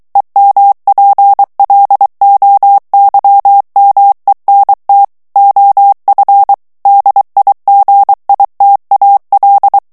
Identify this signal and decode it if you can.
Morse code, heard as a keyed tone, about 23 wpm, 800 Hz. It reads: EMPLOYMENT OF DIGITAL
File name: ANGLAIS texte clair 3 vitesse 22 mots minute